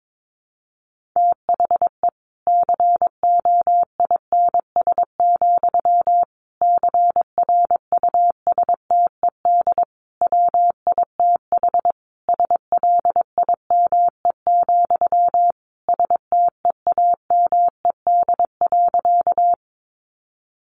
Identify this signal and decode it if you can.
Morse code, heard as a keyed tone, about 22 wpm, 700 Hz. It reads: THE COINS, CRUSTED WITH SLIME, STEAMED.
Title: A Sound Of Thunder 22wpm